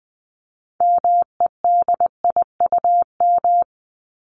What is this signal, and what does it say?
Morse code, heard as a keyed tone, about 20 wpm, 700 Hz. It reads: MEDIUM
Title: Top 751-1000 Words 20wpm